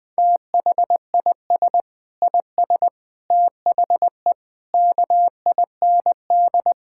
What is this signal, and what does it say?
Morse code, heard as a keyed tone, about 20 wpm, 700 Hz. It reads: THIS IS THE KIND
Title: Sentences with 7-Word Phrases - 20wpm